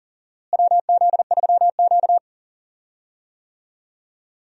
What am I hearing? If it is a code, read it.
Morse code, heard as a keyed tone, about 40 wpm, 700 Hz. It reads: WZ3Q
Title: USA Call Signs - 4 Characters - Repeated 2x - 40wpm